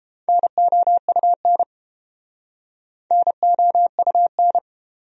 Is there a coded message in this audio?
Yes: Morse code from 0.3 to 4.6 s, heard as a keyed tone, about 29 wpm, 700 Hz. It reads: NOUN NOUN